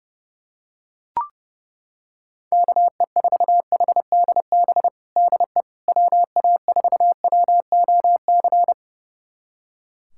Morse code, heard as a keyed tone, about 32 wpm, 700 Hz. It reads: KE4HDB DE WA4WOC
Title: USA Callsign DE Callsign - 6 Characters - 30wpm